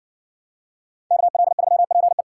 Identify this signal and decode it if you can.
Morse code, heard as a keyed tone, about 57 wpm, 700 Hz. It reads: KL3PE